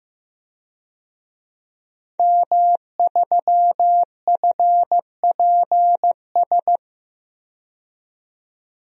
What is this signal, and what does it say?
Morse code, heard as a keyed tone, about 16 wpm, 700 Hz. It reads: M3FPS
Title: DXCC-100 Call Signs - 5 Characters - 15wpm